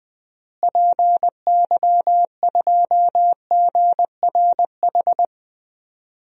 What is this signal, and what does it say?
Morse code, heard as a keyed tone, about 21 wpm, 700 Hz. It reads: PY2GRH
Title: DXCC-100 Call Signs - 6 Characters - Repeated 2x - 20wpm